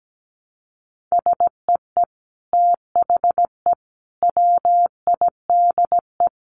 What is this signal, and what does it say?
Morse code, heard as a keyed tone, about 17 wpm, 700 Hz. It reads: SEE THE WIDE
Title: Sentences from Top 900 Words 17wpm